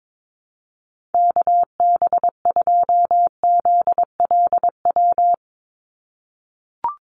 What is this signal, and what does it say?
Morse code, heard as a keyed tone, about 22 wpm, 700 Hz. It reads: KB2ZLW
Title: USA Call Signs 22wpm